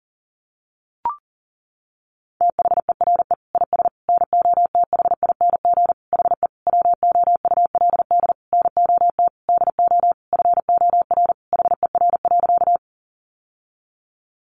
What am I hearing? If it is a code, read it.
Morse code, heard as a keyed tone, about 40 wpm, 700 Hz. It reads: THERE IS NOTHING HE WOULD NOT DO FOR HER.
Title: Sentences from Top 600 Words 40wpm